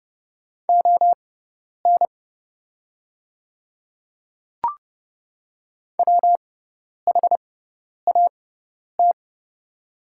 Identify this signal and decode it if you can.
Morse code, heard as a keyed tone, about 30 wpm, spaced as for 10 wpm, 700 Hz. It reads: ON WHAT
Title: 6x Character Spacing - Top 300 Words - 30wpm